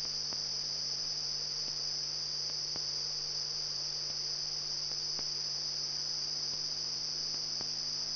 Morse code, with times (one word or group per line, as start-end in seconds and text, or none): none